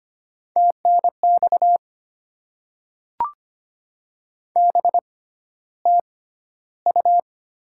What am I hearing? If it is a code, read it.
Morse code, heard as a keyed tone, about 25 wpm, spaced as for 13 wpm, 700 Hz. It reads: TNX BTU
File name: Top 20 QSO Elements - 6x Character Spacing - 25wpm